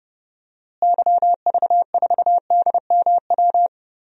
Morse code, heard as a keyed tone, about 30 wpm, 700 Hz. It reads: YV4DMW
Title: DXCC-100 Call Signs - 6 Characters - Repeated 2x - 30wpm